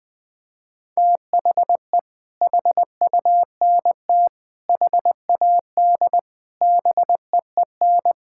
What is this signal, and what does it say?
Morse code, heard as a keyed tone, about 20 wpm, 700 Hz. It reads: THE HUNT HAD BEEN
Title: Sentences from Top 1600 Words 20wpm